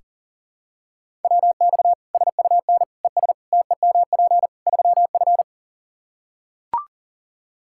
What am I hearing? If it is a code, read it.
Morse code, heard as a keyed tone, about 40 wpm, 700 Hz. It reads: WX SUN ES TEMP 3F
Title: K1USN SST QSO Practice 40wpm